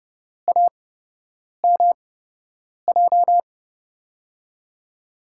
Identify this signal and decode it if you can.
Morse code, heard as a keyed tone, about 30 wpm, spaced as for 8 wpm, 700 Hz. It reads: AMJ
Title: USA Call Signs - 5 Characters - 8x Character Spacing - 30wpm